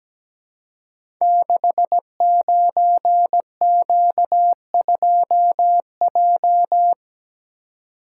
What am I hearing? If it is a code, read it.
Morse code, heard as a keyed tone, about 17 wpm, 700 Hz. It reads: 69Q2J
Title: Code Groups - 5 Letter:Numbers - 17wpm